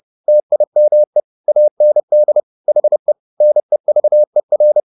Morse code, heard as a keyed tone, about 30 wpm, 600 Hz. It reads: TIME AND HE NEVER
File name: the old man and the sea / the old man and the sea 1 30wpm